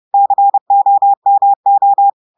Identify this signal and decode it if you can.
Morse code, heard as a keyed tone, about 30 wpm, 800 Hz. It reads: COMO
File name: audios / portugais / portugais texte en clair 1 vitesse 30 mots minutes